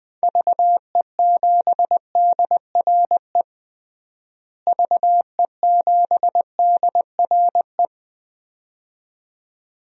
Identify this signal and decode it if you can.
Morse code, heard as a keyed tone, about 21 wpm, 700 Hz. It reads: VE7DRE VE7DRE
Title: DXCC-100 Call Signs - 6 Characters - Repeated 2x - 20wpm